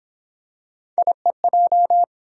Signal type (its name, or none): Morse code